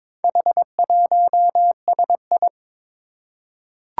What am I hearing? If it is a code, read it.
Morse code, heard as a keyed tone, about 21 wpm, 700 Hz. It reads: H1SI
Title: Four Challenging Letter-Number Combinations 22wpm